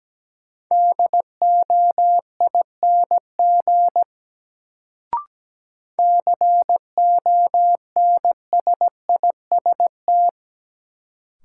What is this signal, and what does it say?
Morse code, heard as a keyed tone, about 17 wpm, 700 Hz. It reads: DOING CONSIST
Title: Sentences from Top 1400 Words - New Word Drill - 17wpm